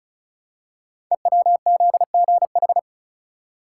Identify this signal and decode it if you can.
Morse code, heard as a keyed tone, about 36 wpm, 690 Hz. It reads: EWZGH
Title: Code Groups - 5 Letters - 35wpm